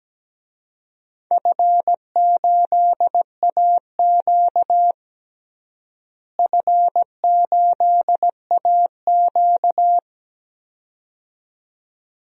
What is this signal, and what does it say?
Morse code, heard as a keyed tone, about 17 wpm, 700 Hz. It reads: F8AQ F8AQ
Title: DXCC-100 Call Signs - 4 Characters - Repeated 2x - 17wpm